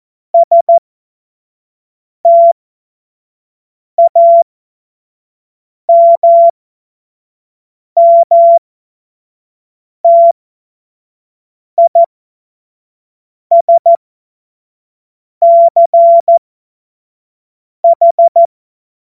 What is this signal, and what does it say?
Morse code, heard as a keyed tone, about 14 wpm, spaced as for 5 wpm, 680 Hz. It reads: STAMMTISCH
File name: Morsekurs / Witze / Witz 18. Januar 2022, 25 F70